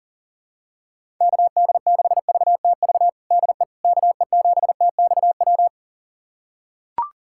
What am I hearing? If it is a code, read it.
Morse code, heard as a keyed tone, about 40 wpm, 700 Hz. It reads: KD6VTV DE KE7TXW
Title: USA Callsign DE Callsign 40wpm